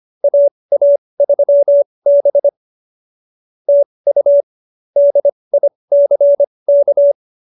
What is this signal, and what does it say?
Morse code, heard as a keyed tone, about 25 wpm, spaced as for 20 wpm, 590 Hz. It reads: AA3B TU DICK
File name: QSO302 20